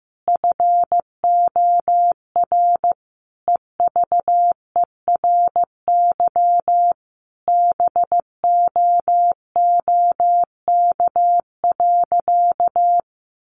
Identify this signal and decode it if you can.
Morse code, heard as a keyed tone, about 15 wpm, 700 Hz. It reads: FOR EVERY BOOK.
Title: Sentences from Top 1500 Words 15wpm